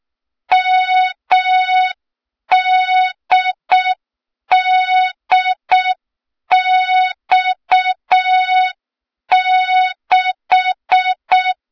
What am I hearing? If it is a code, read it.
Morse code, heard as a keyed tone, about 6 wpm, 3700 Hz. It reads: MDDX6
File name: Morze-HA2MN / hang / 23 GYAKORLAT